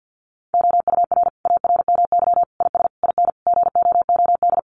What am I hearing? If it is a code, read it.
Morse code, heard as a keyed tone, about 50 wpm, 700 Hz. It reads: OUR ARMY IS IN GOOD
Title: Sentences from Top 1500 Words 50wpm